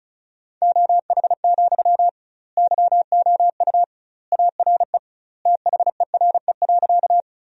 Morse code, heard as a keyed tone, about 36 wpm, 700 Hz. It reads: OH, YOU ARE THERE.
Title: 2x Word-Spacing - Sentences from Top 500 Words - 35wpm